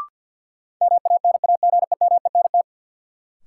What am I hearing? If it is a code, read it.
Morse code, heard as a keyed tone, about 51 wpm, 700 Hz. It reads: MANAGEMENT
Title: Sentences from Top 1200 Words - New Word Drill Alt - 50wpm